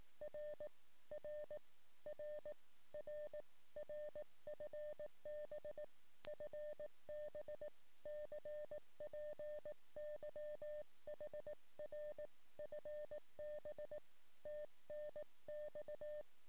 Morse code, heard as a keyed tone, about 18 wpm, 600 Hz. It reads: R R R R R FB FB CPY HR FB TNX
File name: sm7lcb / 08 sm5le-p jo98ax 23cm